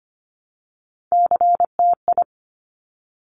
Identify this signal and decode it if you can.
Morse code, heard as a keyed tone, about 25 wpm, 700 Hz. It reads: CTI
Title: Top Three Letter Combinations in Words 25wpm